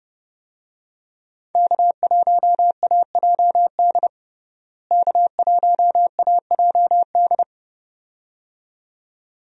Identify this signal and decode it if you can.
Morse code, heard as a keyed tone, about 30 wpm, 700 Hz. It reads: K1AJD K1AJD
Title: USA Call Signs - 5 Characters - Repeated 2x - 30wpm